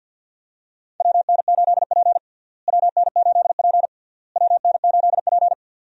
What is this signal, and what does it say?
Morse code, heard as a keyed tone, about 48 wpm, 700 Hz. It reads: WN8P WN8P WN8P
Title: USA Call Signs - 4 Characters - Repeated 3x - 50wpm